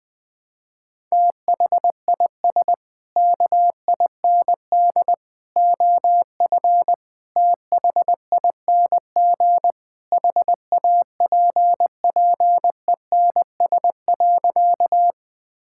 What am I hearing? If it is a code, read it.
Morse code, heard as a keyed tone, about 20 wpm, 700 Hz. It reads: THIS KIND OF THING HAPPENS.
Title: Sentences with 3-Word Phrases - 20wpm